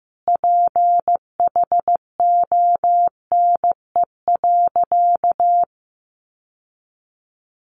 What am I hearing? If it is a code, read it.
Morse code, heard as a keyed tone, about 15 wpm, 700 Hz. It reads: PHONE.
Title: Sentences from Top 1100 Words 15wpm